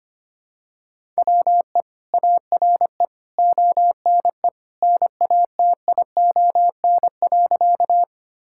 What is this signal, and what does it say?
Morse code, heard as a keyed tone, about 25 wpm, 700 Hz. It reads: WE ARE ONE NATION.
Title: Sentences from Top 1000 Words - Review Part 6 - 25wpm